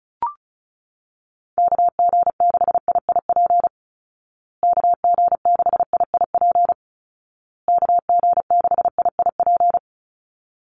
Morse code, heard as a keyed tone, about 34 wpm, 700 Hz. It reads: KG6IIP KG6IIP KG6IIP
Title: Call Signs Repeated 2x - 35wpm